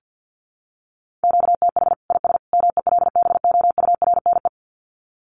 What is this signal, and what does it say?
Morse code, heard as a keyed tone, about 50 wpm, 700 Hz. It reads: QTH IS MELBOURNE
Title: Common QSO Phrases 50wpm